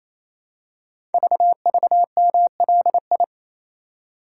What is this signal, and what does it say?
Morse code, heard as a keyed tone, about 27 wpm, 690 Hz. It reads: VVMLI
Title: Code Groups - 5 Letters - 28wpm